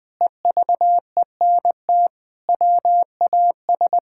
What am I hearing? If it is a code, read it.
Morse code, heard as a keyed tone, about 19 wpm, 700 Hz. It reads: EVENT WAS
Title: Sentences with 7-Word Phrases - 20wpm